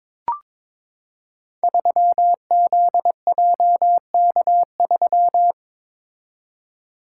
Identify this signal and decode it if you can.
Morse code, heard as a keyed tone, about 22 wpm, 700 Hz. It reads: 3ZJK3
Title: Code Groups - 5 Letter:Numbers - 22wpm